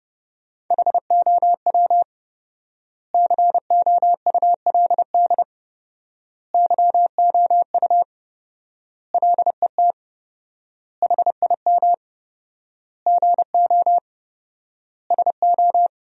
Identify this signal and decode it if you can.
Morse code, heard as a keyed tone, about 29 wpm, 700 Hz. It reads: HOW COULD YOU LET HIM GO SO
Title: 4x Word-Spacing - Sentences from Top 300 Words - 30wpm